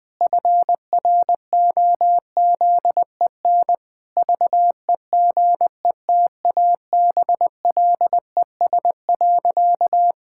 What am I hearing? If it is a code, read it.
Morse code, heard as a keyed tone, about 20 wpm, 700 Hz. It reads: FROZEN VEGETABLES.
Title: Pangram Challenge v2 - 20wpm